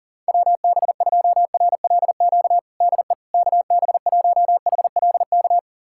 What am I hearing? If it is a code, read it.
Morse code, heard as a keyed tone, about 40 wpm, 700 Hz. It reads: WB2RLQ DE KB1HLK
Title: USA Callsign DE Callsign - 6 Characters - 40wpm